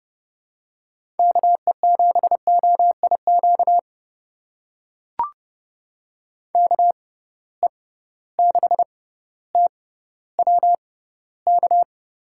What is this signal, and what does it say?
Morse code, heard as a keyed tone, about 30 wpm, spaced as for 15 wpm, 700 Hz. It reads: KE7OIQ KE6TWK
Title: USA Call Signs - 6 Characters - 6x Character Spacing - 30wpm